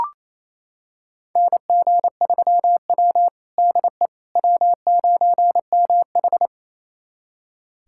Morse code, heard as a keyed tone, about 27 wpm, 690 Hz. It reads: NG3W DE W9MH
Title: USA Callsign DE Callsign - 4 Characters - 28wpm